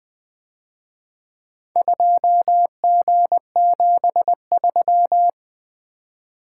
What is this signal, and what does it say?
Morse code, heard as a keyed tone, about 24 wpm, 700 Hz. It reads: 2G73
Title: Speed-Racing - Four Letter-Number Combinations - 20wpm